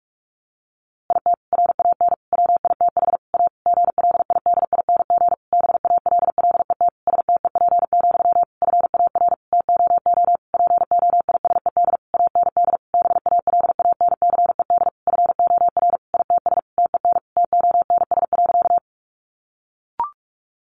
Morse code, heard as a keyed tone, about 44 wpm, 700 Hz. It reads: IT RAN WITH A GLIDING BALLET STEP, FAR TOO POISED AND BALANCED FOR ITS TEN TONS.
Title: A Sound Of Thunder 45wpm